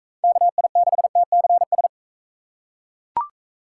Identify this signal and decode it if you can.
Morse code, heard as a keyed tone, about 39 wpm, 700 Hz. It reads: KI6TCS